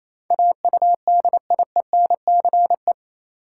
Morse code, heard as a keyed tone, about 28 wpm, 700 Hz. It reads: AUDIENCE
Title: Sentences from Top 800 Words - New Word Drill - 28wpm